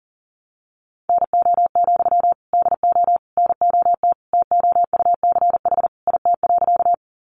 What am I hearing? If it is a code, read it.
Morse code, heard as a keyed tone, about 40 wpm, 700 Hz. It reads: NO, DO NOT TOUCH IT.
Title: Sentences from Top 1200 Words 40wpm